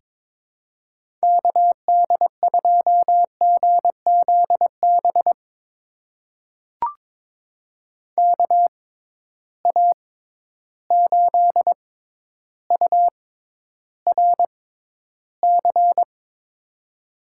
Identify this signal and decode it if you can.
Morse code, heard as a keyed tone, about 22 wpm, spaced as for 11 wpm, 700 Hz. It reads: KD2GZB KA8URC
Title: USA Call Signs - 6 Characters - 6x Character Spacing - 22wpm